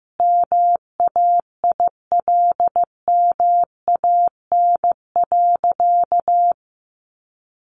Morse code, heard as a keyed tone, about 15 wpm, 700 Hz. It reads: MAILMAN.
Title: Additional Easy Phrases 15wpm